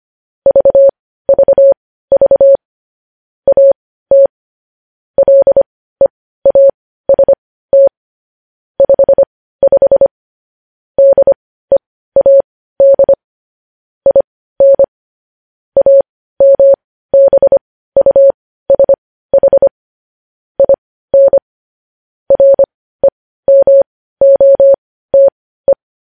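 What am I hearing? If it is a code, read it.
Morse code, heard as a keyed tone, about 25 wpm, spaced as for 15 wpm, 550 Hz. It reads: VVV AT LEAST 55 DEAD IN AMBUSH IN REMOTE
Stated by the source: Morse code news headlines sent at 15 words per minute.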